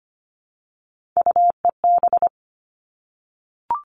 Morse code, heard as a keyed tone, about 25 wpm, 700 Hz. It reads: UEB
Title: All Cut Numbers - 3-Digit 25wpm